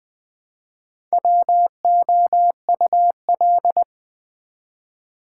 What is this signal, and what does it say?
Morse code, heard as a keyed tone, about 20 wpm, 700 Hz. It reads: WOUL
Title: Build A Word - 1 Letter At A Time - 5-Letter Words - 20wpm